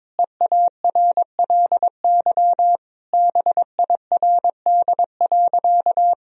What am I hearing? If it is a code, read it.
Morse code, heard as a keyed tone, about 22 wpm, 700 Hz. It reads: EARLY BIRD.
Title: Sentences from Top 1000 Words - Review Part 2 - 22wpm